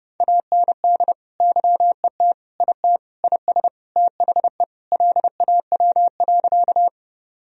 Morse code, heard as a keyed tone, about 30 wpm, 700 Hz. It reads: AND YET IT IS THE LAW.
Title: Sentences from Top 1000 Words - Review Part 3 - 30wpm